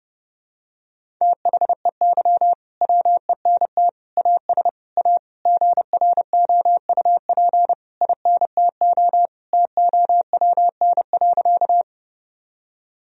Morse code, heard as a keyed tone, about 32 wpm, 700 Hz. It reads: THEY WENT AS A GROUP INTO TOWN.
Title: Sentences from Top 1000 Words - Review Part 1 - 30wpm